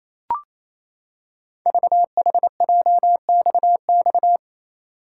Morse code, heard as a keyed tone, about 28 wpm, 700 Hz. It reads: VHJXX
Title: Code Groups - 5 Letters - 28wpm